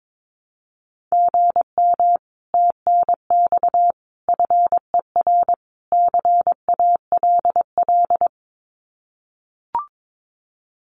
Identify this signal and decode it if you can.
Morse code, heard as a keyed tone, about 22 wpm, 700 Hz. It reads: GM TNX FER CALL
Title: Common QSO Phrases Encore 22wpm